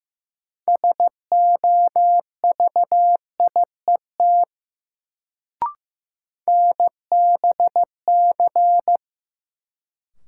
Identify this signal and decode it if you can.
Morse code, heard as a keyed tone, about 15 wpm, 700 Hz. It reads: SOVIET NBC
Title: Sets of 5 Words Encore - New Word Drill - 15wpm